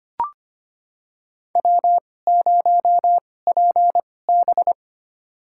Morse code, heard as a keyed tone, about 25 wpm, spaced as for 18 wpm, 700 Hz. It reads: W0PB
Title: USA Call Signs - 2x Character Spacing - 25wpm